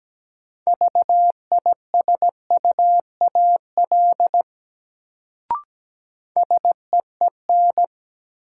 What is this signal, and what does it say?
Morse code, heard as a keyed tone, about 17 wpm, 700 Hz. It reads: VISUAL SEEN
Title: Sets of 5 Words Encore - New Word Drill - 17wpm